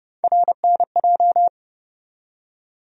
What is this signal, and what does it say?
Morse code, heard as a keyed tone, about 30 wpm, 700 Hz. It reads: RNJ
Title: USA Call Sign Postfixes - 3 Letters 30wpm